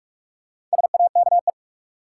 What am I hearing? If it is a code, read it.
Morse code, heard as a keyed tone, about 46 wpm, 700 Hz. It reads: SAKE
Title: Sets of 4 Words - New Word Drill - 45wpm